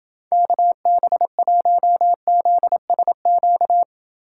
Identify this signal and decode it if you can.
Morse code, heard as a keyed tone, about 28 wpm, 700 Hz. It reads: KB1ZSQ